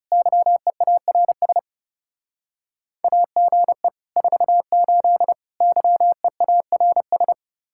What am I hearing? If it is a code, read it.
Morse code, heard as a keyed tone, about 36 wpm, 700 Hz. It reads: YEARS AGE 48 YEARS
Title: Speed-Racing Common QSO Phrases Encore - 30wpm